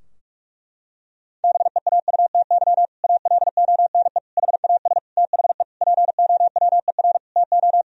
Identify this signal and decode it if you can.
Morse code, heard as a keyed tone, about 46 wpm, 700 Hz. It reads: BEAUTY ALONE HAS THE POWER TO
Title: Sentences from Top 1000 Words - Review Part 2 - 45wpm